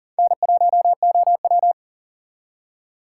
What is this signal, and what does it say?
Morse code, heard as a keyed tone, about 51 wpm, 700 Hz. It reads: N1OW